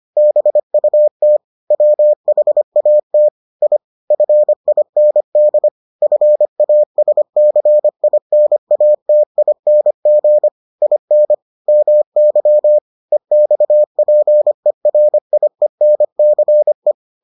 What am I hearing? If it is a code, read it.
Morse code, heard as a keyed tone, about 25 wpm, 600 Hz. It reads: BUT WHAT I FIND FASCINATING IN MY EXPERIENCE